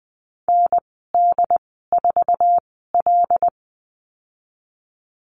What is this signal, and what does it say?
Morse code, heard as a keyed tone, about 20 wpm, spaced as for 14 wpm, 700 Hz. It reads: ND4L
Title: 076 Lesson 13g - Letter L - Call Signs 20wpm